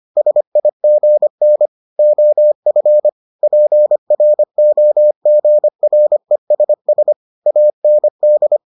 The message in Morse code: SIGN OF PROGRESS AND

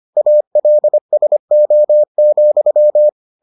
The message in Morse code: ALSO,

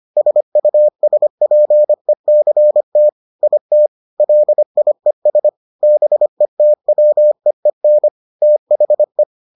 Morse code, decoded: SUSPECT IT LIES BETWEEN THE

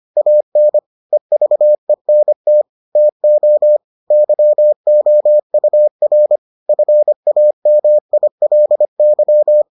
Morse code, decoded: AN EVENT TO YOUR FAMILY

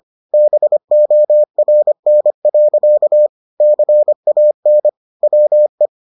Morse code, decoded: BORN. CAN WE

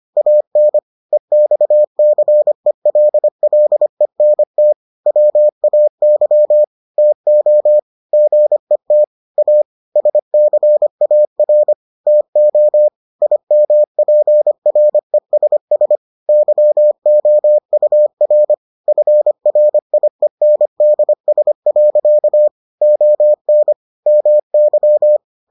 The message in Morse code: AN EXCELLENT WAY TO GET A SCAR TO IMPRESS YOUR FRIENDS. ON MY

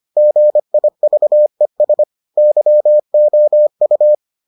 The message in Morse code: GIVES YOU

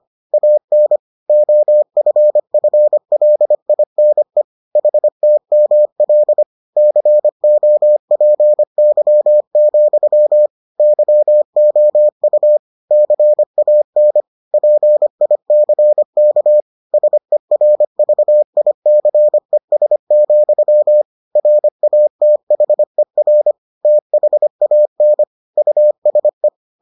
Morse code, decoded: AN OFFLINE HTML COPY, YOU CAN PICK SERVICES, RATHER THAN USE

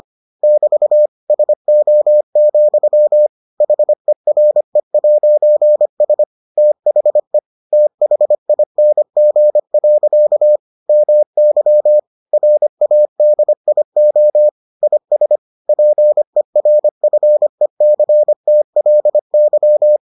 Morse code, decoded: <BT> SO, HERE'S THE THING. MY RADIO IS PERFECTLY

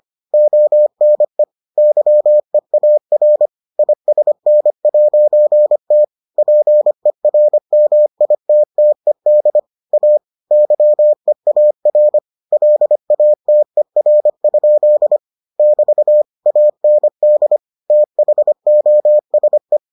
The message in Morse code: ONE YEAR ISN'T PERMITTED A YEAR LATER? <BT> AND THOSE

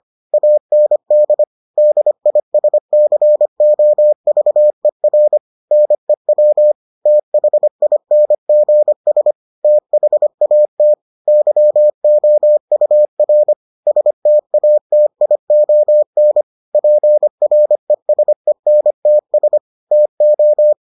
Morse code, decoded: AND DISCOVER NEW THINGS THAT YOUR STATION PRESENTS TO

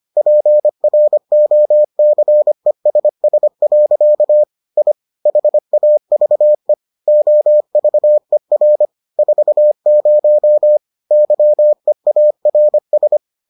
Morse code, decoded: PROCESS. I HAVE OVER 40 YEARS

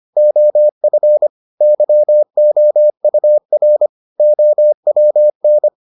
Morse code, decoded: OF YOUR OWN

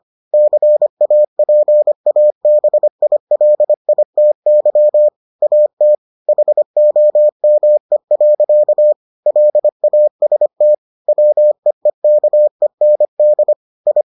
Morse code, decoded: CAPABILITY AT HOME. LAST WEEKEND I

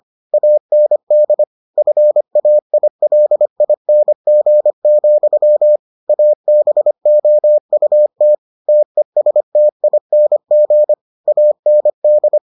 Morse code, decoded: AND FAILING, ABOUT TESTING AND

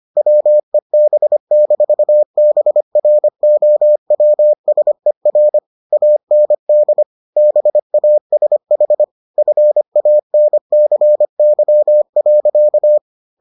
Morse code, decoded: WEB-BROWSER AND BASH FANCY.

